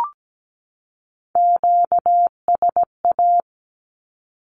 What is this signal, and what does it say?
Morse code, heard as a keyed tone, about 17 wpm, 700 Hz. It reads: QSA
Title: Q-Codes 17wpm